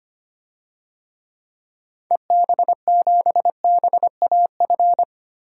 Morse code, heard as a keyed tone, about 25 wpm, 700 Hz. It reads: EB7BAF
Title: DXCC-100 Call Signs - 6 Characters - Repeated 3x - 25wpm